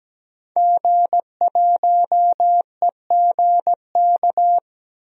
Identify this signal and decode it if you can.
Morse code, heard as a keyed tone, about 17 wpm, 700 Hz. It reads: G1EGK
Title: DXCC-100 Call Signs - 5 Characters - 17wpm